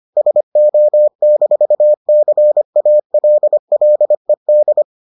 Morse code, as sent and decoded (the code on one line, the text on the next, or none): ... --- -....- -.-. .- .-.. .-.. -..
SO-CALLED